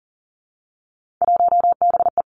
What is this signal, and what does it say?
Morse code, heard as a keyed tone, about 40 wpm, 700 Hz. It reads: 1BE
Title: Three Challenging Letter-Number Combinations 40wpm